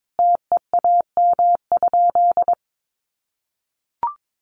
Morse code, heard as a keyed tone, about 22 wpm, 700 Hz. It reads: TEAM?
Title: Sentences from Top 1500 Words 22wpm